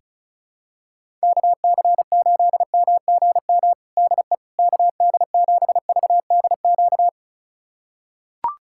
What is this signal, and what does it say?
Morse code, heard as a keyed tone, about 34 wpm, 700 Hz. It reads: KC8MGM DE KD7VDQ
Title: USA Callsign DE Callsign - 6 Characters - 35wpm